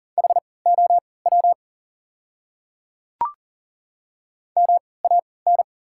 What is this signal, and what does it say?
Morse code, heard as a keyed tone, about 40 wpm, spaced as for 23 wpm, 700 Hz. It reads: HOW MAN
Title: 3x Character Spacing - Top 300 Words - 40wpm